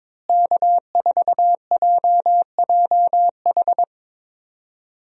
Morse code, heard as a keyed tone, about 22 wpm, 700 Hz. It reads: K4JJH
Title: USA Call Signs - 5 Characters - Repeated 2x - 22wpm